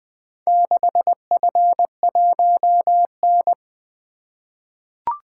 Morse code, heard as a keyed tone, about 20 wpm, 700 Hz. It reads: 6F1N
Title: Four Letter-Number Combinations - 20wpm